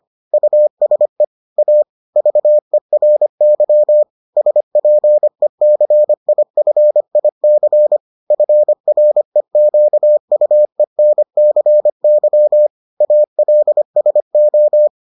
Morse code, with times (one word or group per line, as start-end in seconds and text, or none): 0.3-1.2 USE
1.6-1.8 A
2.2-4.0 VERY
4.4-8.0 SPECIFIC
8.3-12.7 FREQUENCY
13.0-14.9 ALSO